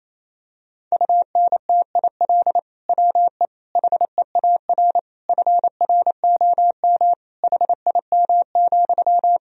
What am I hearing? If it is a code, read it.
Morse code, heard as a keyed tone, about 28 wpm, 700 Hz. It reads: UNTIL WE HEAR FROM HIM,
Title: Sentences from Top 1000 Words - Review Part 1 - 28wpm